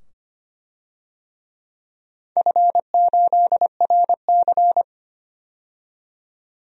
Morse code, heard as a keyed tone, about 25 wpm, 700 Hz. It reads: F8RC
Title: DXCC-100 Call Signs - 4 Characters - 25wpm